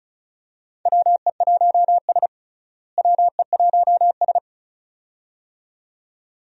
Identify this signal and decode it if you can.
Morse code, heard as a keyed tone, about 36 wpm, 700 Hz. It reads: WE1S WE1S
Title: USA Call Signs - 4 Characters - Repeated 2x - 35wpm